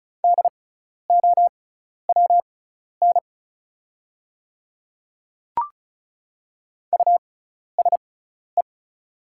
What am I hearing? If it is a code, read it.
Morse code, heard as a keyed tone, about 36 wpm, spaced as for 12 wpm, 700 Hz. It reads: DOWN USE
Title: 6x Character Spacing - Top 100 Words - 35wpm